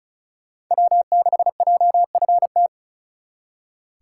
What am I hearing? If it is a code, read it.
Morse code, heard as a keyed tone, about 45 wpm, 700 Hz. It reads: W6JFT